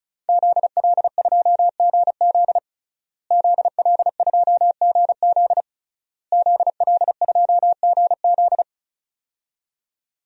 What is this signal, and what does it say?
Morse code, heard as a keyed tone, about 36 wpm, 700 Hz. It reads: ZL2GZ ZL2GZ ZL2GZ
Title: DXCC-100 Call Signs - 5 Characters - Repeated 3x - 35wpm